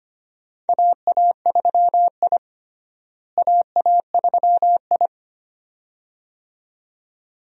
Morse code, heard as a keyed tone, about 25 wpm, 700 Hz. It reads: AA3I AA3I
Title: USA Call Signs - 4 Characters - Repeated 2x - 25wpm